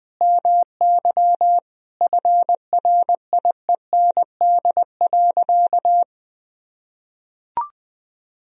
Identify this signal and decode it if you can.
Morse code, heard as a keyed tone, about 20 wpm, 700 Hz. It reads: MY FRIEND.
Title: Sentences from Top 300 Words and Abbreviations 20wpm